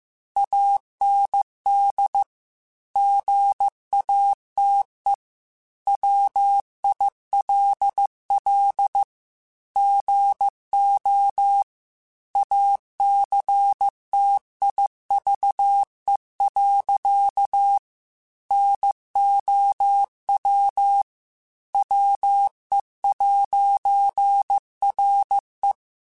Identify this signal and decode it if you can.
Morse code, heard as a keyed tone, about 15 wpm, 800 Hz. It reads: AND GATE WILL GO ACTIVE. NOW WE'RE
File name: Technical Subjects / Technical Subjects-mp3s / AND Gate @13